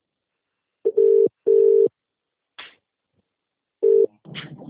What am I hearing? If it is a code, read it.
Ringback tone (UK), 400 + 450 Hz.